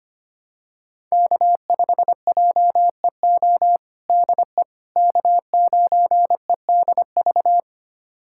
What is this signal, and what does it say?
Morse code, heard as a keyed tone, about 25 wpm, 700 Hz. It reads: K5JEO DE K9EDV
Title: USA Callsign DE Callsign - 5 Characters - 25wpm